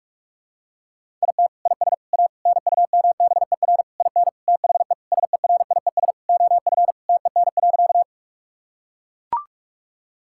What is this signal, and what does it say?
Morse code, heard as a keyed tone, about 41 wpm, 700 Hz. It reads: IT IS A NUMBER IN THE SERIES OF TEN.
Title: Sentences from Top 1000 Words - Review Part 5 - 45wpm